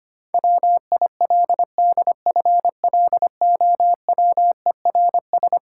Morse code, heard as a keyed tone, about 25 wpm, 700 Hz. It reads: WILDFLOWERS
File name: practice / 25wpm / Difficult Sentences with Hints - 25wpm